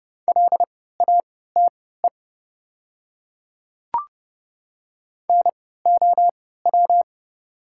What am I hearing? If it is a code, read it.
Morse code, heard as a keyed tone, about 30 wpm, spaced as for 17 wpm, 700 Hz. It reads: LATE NOW
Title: 3x Character Spacing - Top 300 Words - 30wpm